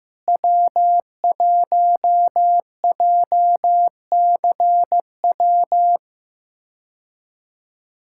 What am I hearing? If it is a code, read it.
Morse code, heard as a keyed tone, about 16 wpm, 700 Hz. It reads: W1JCW
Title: K1USN SST QSO Practice - 15wpm